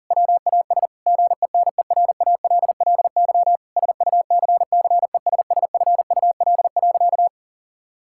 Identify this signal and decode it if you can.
Morse code, heard as a keyed tone, about 40 wpm, 700 Hz. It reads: WAS GENERALLY SUCCESSFUL.
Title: Sentences from Top 1000 Words - Review Part 6 - 40wpm